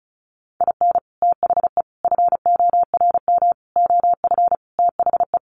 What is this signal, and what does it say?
Morse code, heard as a keyed tone, about 34 wpm, 700 Hz. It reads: IN THE FORM OF THE
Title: Sets of 5 Words Encore2 35wpm